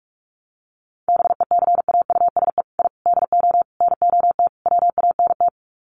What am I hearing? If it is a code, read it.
Morse code, heard as a keyed tone, about 44 wpm, 700 Hz. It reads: BECAUSE I DO NOT WANT
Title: Sets of 5 Words Encore 45wpm